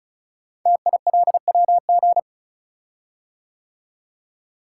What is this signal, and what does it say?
Morse code, heard as a keyed tone, about 36 wpm, 700 Hz. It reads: TILWG
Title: Code Groups - 5 Letters - 35wpm